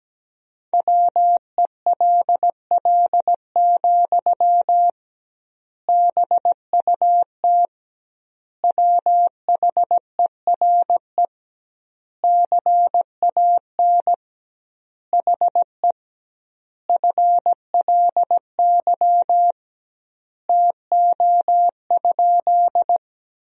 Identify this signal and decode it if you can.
Morse code, heard as a keyed tone, about 17 wpm, 700 Hz. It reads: WELL, BUT WHERE CAN HE FLY TO?
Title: 2x Word-Spacing - Sentences from Top 500 Words - 17wpm